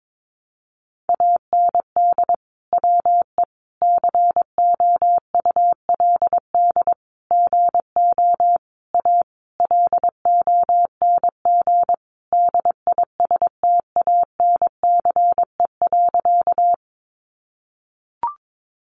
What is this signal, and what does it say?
Morse code, heard as a keyed tone, about 22 wpm, 700 Hz. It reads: AND WE COULD GO A LONG DISTANCE.
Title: Sentences from Top 1200 Words 22wpm